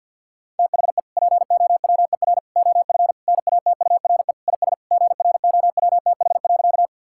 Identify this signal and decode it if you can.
Morse code, heard as a keyed tone, about 50 wpm, 700 Hz. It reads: THE POWER OF NATURE IS GROWTH.
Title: Sentences from Top 1000 Words - Review Part 5 - 50wpm